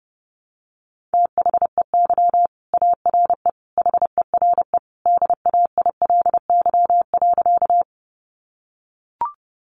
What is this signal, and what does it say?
Morse code, heard as a keyed tone, about 30 wpm, 700 Hz. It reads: THEY ARE HERE DAILY.
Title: Sentences from Top 1100 Words 30wpm